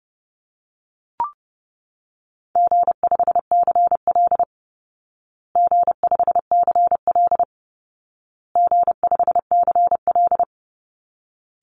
Morse code, heard as a keyed tone, about 30 wpm, 700 Hz. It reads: G5CL G5CL G5CL
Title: Easy Call Signs Repeated 3x - 30wpm